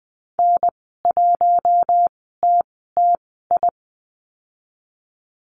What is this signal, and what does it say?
Morse code, heard as a keyed tone, about 20 wpm, spaced as for 14 wpm, 700 Hz. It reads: N1TTI
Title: 042 Lesson 8g - Number 1 - Call Signs 20wpm